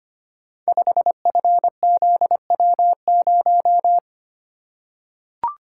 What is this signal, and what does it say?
Morse code, heard as a keyed tone, about 25 wpm, 700 Hz. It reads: 5FZW0
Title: Code Groups - 5 Letter:Numbers - 25wpm